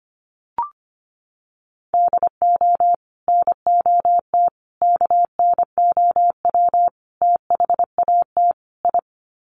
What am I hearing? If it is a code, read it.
Morse code, heard as a keyed tone, about 25 wpm, 700 Hz. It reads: DO NOT KNOW THAT I
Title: Sets of 5 Words Encore2 25wpm